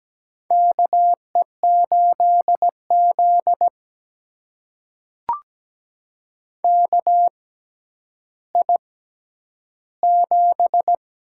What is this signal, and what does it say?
Morse code, heard as a keyed tone, about 17 wpm, spaced as for 10 wpm, 700 Hz. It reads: KE8Z KI7
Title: USA Call Signs - 6x Character Spacing - 17wpm